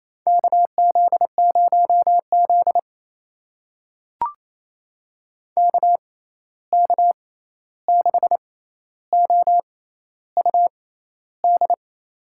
Morse code, heard as a keyed tone, about 27 wpm, spaced as for 13 wpm, 700 Hz. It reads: KZ0Z KK6OUD
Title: USA Call Signs - 6x Character Spacing - 28wpm